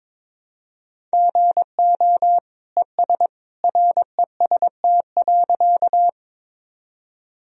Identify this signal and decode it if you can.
Morse code, heard as a keyed tone, about 22 wpm, 700 Hz. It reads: GO ES REST.
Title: Sentences from Top 500 Words and Abbreviations 22wpm